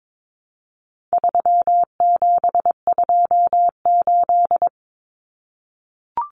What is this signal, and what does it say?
Morse code, heard as a keyed tone, about 22 wpm, 700 Hz. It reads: 3728
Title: Common Cut Numbers - 4-Digit 22wpm